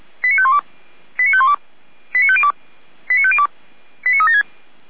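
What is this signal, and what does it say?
Five-tone selective call (ZVEI), heard as quick runs of five tones, tones of about 70 ms. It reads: 89512 89512 89502 89502 89376
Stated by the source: ZVEI five-tone call received by SDR, from the Feuerwehr Orlamünde fire brigade server.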